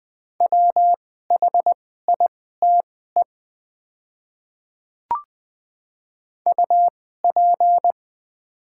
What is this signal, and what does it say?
Morse code, heard as a keyed tone, about 20 wpm, spaced as for 14 wpm, 700 Hz. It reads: WHITE UP
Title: 2x Character Spacing - Top 300 Words - 20wpm